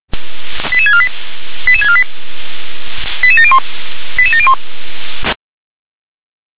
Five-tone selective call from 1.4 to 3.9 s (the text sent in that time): 88648 88701